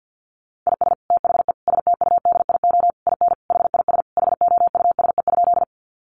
Morse code, heard as a keyed tone, about 50 wpm, 700 Hz. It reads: IS THE STUDIO IN HIS HOUSE?
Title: Sentences from Top 1200 Words 50wpm